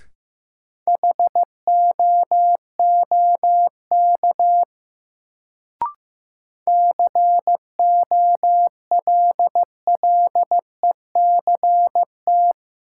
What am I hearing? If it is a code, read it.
Morse code, heard as a keyed tone, about 15 wpm, 700 Hz. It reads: HOOK COLLECT
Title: Sentences from Top 1600 Words - New Word Drill Alt - 15wpm